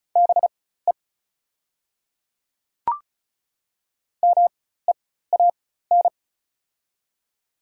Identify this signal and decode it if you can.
Morse code, heard as a keyed tone, about 34 wpm, spaced as for 16 wpm, 700 Hz. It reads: BE MEAN
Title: 4x Character Spacing - Top 200 Words - 35wpm